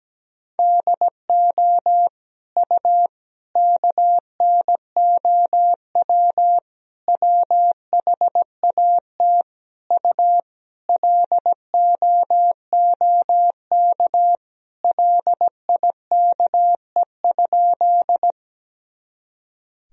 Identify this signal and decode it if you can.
Morse code, heard as a keyed tone, about 17 wpm, 700 Hz. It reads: DO U KNOW WHAT U LOOK LIKE?
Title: Sentences from Top 100 Words and Abbreviations 17wpm